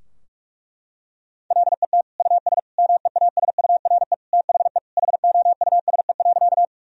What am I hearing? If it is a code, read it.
Morse code, heard as a keyed tone, about 46 wpm, 700 Hz. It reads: LET US MEASURE THE HOUSE.
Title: Sentences from Top 1000 Words - Review Part 2 - 45wpm